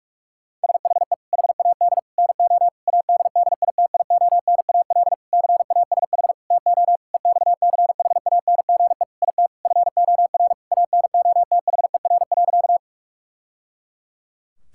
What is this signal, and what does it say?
Morse code, heard as a keyed tone, about 46 wpm, 700 Hz. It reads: SHE HAD NO ADDITIONAL CASH TO EXCHANGE IT FOR ANOTHER.
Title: Sentences from Top 1000 Words - Review Part 6 - 45wpm